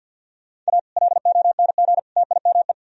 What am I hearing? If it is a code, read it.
Morse code, heard as a keyed tone, about 61 wpm, 700 Hz. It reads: A LONG TIME
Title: Speed-Racing - Sets of 3 Words 50wpm